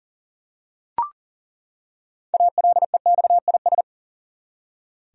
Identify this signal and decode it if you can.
Morse code, heard as a keyed tone, about 40 wpm, 700 Hz. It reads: ALEXIS
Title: USA Common Names - 40wpm